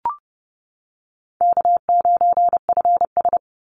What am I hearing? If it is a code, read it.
Morse code, heard as a keyed tone, about 30 wpm, 700 Hz. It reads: K9FS
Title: Easy Call Signs - 30wpm